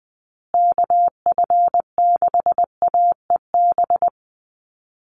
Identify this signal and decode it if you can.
Morse code, heard as a keyed tone, about 20 wpm, 700 Hz. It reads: KF6AEB
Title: USA Call Signs Repeated 3x 20wpm